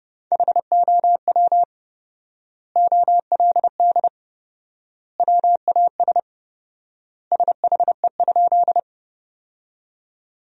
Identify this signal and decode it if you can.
Morse code, heard as a keyed tone, about 30 wpm, 700 Hz. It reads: HOW OLD WAS SHE?
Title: 4x Word-Spacing - Sentences from Top 200 Words - 30wpm